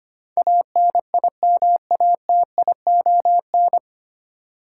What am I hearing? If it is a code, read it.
Morse code, heard as a keyed tone, about 25 wpm, 700 Hz. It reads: ANIMATION
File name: Nine Letter Words - 25wpm